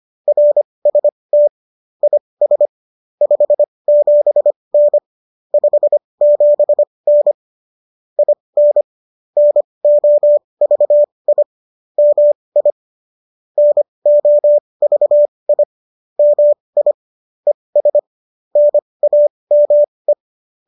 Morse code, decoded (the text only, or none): RST IS 57N 57N IN NOVI MI NOVI MI ES NAME